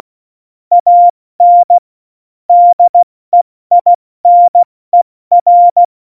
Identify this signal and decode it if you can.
Morse code, heard as a keyed tone, about 16 wpm, 700 Hz. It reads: AN DEINER
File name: Morsekurs / Witze / Witz 26. Januar 2021, 70 F80